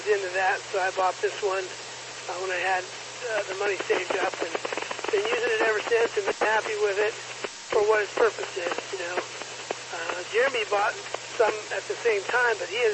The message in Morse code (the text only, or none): none